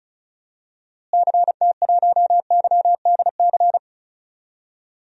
Morse code, heard as a keyed tone, about 36 wpm, 700 Hz. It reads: CT1YDC
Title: DXCC-100 Call Signs - 6 Characters - Repeated 3x - 35wpm